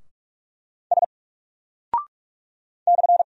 Morse code, heard as a keyed tone, about 44 wpm, 700 Hz. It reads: S /
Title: Single Character (Letter:Number:Punctuation:Prosign) 45wpm